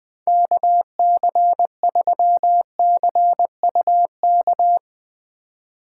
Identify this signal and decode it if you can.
Morse code, heard as a keyed tone, about 20 wpm, 700 Hz. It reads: KC3CUK
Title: Speed Racing - USA 2x3 Call Signs 17wpm